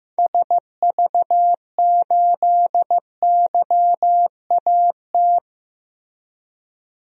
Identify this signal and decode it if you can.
Morse code, heard as a keyed tone, about 15 wpm, 700 Hz. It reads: SV8YAT
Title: DXCC-100 Call Signs - 6 Characters - Repeated 2x - 15wpm